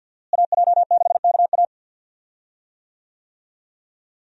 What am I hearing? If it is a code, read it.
Morse code, heard as a keyed tone, about 52 wpm, 700 Hz. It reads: AJ6KA
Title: USA Call Signs - 5 Characters - Repeated 2x - 50wpm